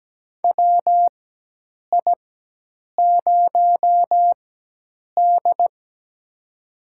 Morse code, heard as a keyed tone, about 17 wpm, spaced as for 8 wpm, 700 Hz. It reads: WI0D
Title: USA Call Signs - 4 Characters - 4x Character Spacing - 17wpm